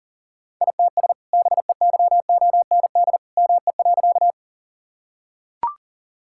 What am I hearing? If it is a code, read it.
Morse code, heard as a keyed tone, about 40 wpm, 700 Hz. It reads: ITS BEYOND ME.
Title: Sentences from Top 1000 Words - Review Part 6 - 40wpm